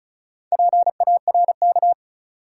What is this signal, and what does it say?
Morse code, heard as a keyed tone, about 34 wpm, 700 Hz. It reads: PARK
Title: Sentences from Top 1300 Words - New Word Drill - 35wpm